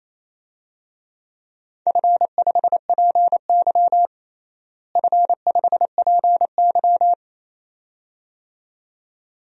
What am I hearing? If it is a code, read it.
Morse code, heard as a keyed tone, about 28 wpm, 700 Hz. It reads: F5PY F5PY
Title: DXCC-100 Call Signs - 4 Characters - Repeated 2x - 28wpm